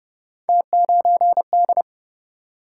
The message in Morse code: T9D

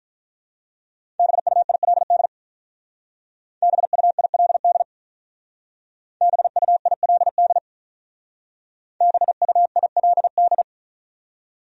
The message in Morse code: BUILD BUILD BUILD BUILD

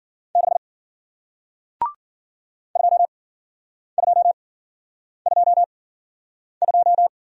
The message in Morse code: 6 2222

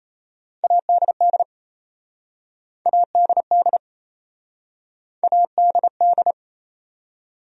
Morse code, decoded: ADD ADD ADD

